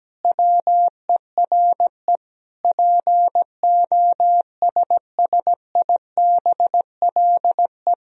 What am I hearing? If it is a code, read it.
Morse code, heard as a keyed tone, about 17 wpm, 700 Hz. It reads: WERE POSSIBLE
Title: Sentences from Top 500 Words and Abbreviations 17wpm